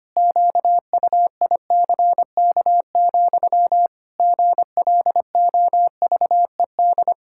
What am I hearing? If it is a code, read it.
Morse code, heard as a keyed tone, about 25 wpm, 700 Hz. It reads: QUICK, GLOVED
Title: Pangram Challenge v2 - 25wpm